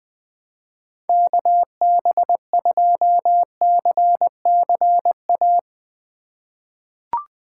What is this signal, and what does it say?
Morse code, heard as a keyed tone, about 20 wpm, 700 Hz. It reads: KB2CCA
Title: USA Call Signs - 6 Characters - 8x Character Spacing - 20wpm